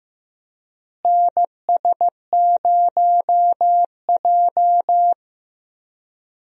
Morse code, heard as a keyed tone, about 15 wpm, 700 Hz. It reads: NS0J
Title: USA Call Signs - 4 Characters - Repeated 2x - 15wpm